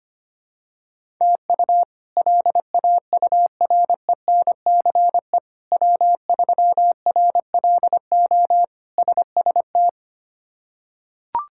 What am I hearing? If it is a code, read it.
Morse code, heard as a keyed tone, about 25 wpm, 700 Hz. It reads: TU LAURENCE W3RLO SST
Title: K1USN SST QSO Practice - 25wpm